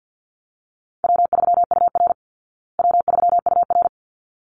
Morse code, heard as a keyed tone, about 49 wpm, 700 Hz. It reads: W3UR W3UR
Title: Call Signs Repeated 3x - 50wpm